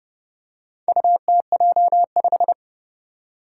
Morse code, heard as a keyed tone, about 30 wpm, 700 Hz. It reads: UTJ5
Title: Four Challenging Letter-Number Combinations 30wpm